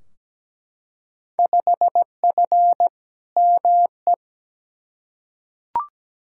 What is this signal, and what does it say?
Morse code, heard as a keyed tone, about 17 wpm, 700 Hz. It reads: 5F ME
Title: Field Day Exchanges Encore - 17wpm